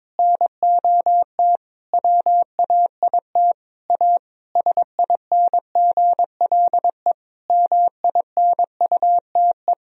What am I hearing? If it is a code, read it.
Morse code, heard as a keyed tone, about 22 wpm, 700 Hz. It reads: NOT WAIT A SINGLE MINUTE
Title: Sentences from Top 1000 Words - Review Part 4 - 22wpm